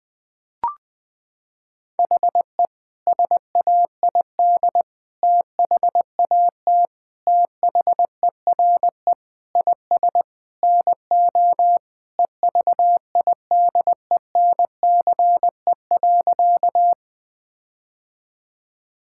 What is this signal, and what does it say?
Morse code, heard as a keyed tone, about 20 wpm, 700 Hz. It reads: HE SAID THAT THERE IS NO EVIDENCE.
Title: Sentences with 4-Word Phrases - 20wpm